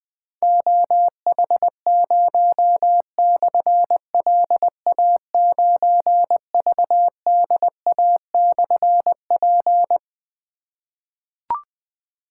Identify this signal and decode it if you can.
Morse code, heard as a keyed tone, about 20 wpm, 700 Hz. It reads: OH0/LA9VDA/P
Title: Difficult Call Signs 20wpm